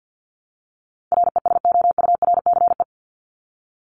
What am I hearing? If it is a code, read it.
Morse code, heard as a keyed tone, about 50 wpm, 700 Hz. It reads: RESOURCE
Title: ES:S Suffix - Top 100 Words with Base Words 50wpm